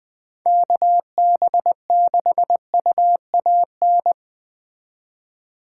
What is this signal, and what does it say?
Morse code, heard as a keyed tone, about 20 wpm, 700 Hz. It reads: KB6UAN
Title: Speed Racing - USA 2x3 Call Signs 17wpm